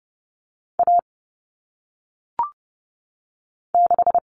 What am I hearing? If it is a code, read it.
Morse code, heard as a keyed tone, about 30 wpm, 700 Hz. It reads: A 6
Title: Common Cut Numbers - 1-Digit 30wpm